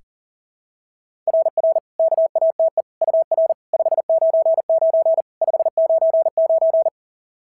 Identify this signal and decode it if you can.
Morse code, heard as a keyed tone, about 39 wpm, 650 Hz. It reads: RR KATE UR 599 599
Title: SOTA QSO Practice 40wpm